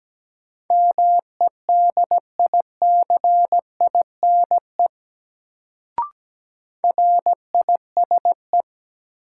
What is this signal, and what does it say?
Morse code, heard as a keyed tone, about 17 wpm, 700 Hz. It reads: MEDICINE RISE
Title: Sentences from Top 900 Words - New Word Drill - 17wpm